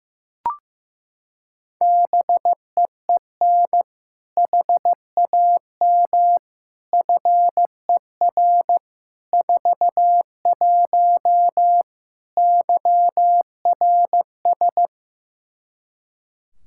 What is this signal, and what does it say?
Morse code, heard as a keyed tone, about 16 wpm, 700 Hz. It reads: BEEN HAM FER 41 YRS
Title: K1USN SST QSO Practice 15wpm